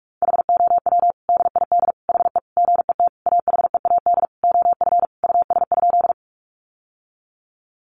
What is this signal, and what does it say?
Morse code, heard as a keyed tone, about 46 wpm, 700 Hz. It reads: HOW DID HE GET AHEAD OF US?
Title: Sentences from Top 1100 Words 45wpm